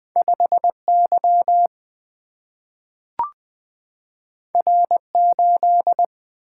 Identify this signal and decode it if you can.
Morse code, heard as a keyed tone, about 20 wpm, 700 Hz. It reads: 5Y R8
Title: Single-Letter Single-Number 20wpm